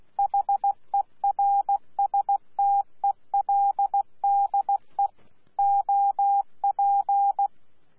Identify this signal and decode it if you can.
Morse code, heard as a keyed tone, about 16 wpm, 800 Hz. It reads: HERSTELDE OP
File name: Z PI4VRZ / PI4VRZ 14-jun-2025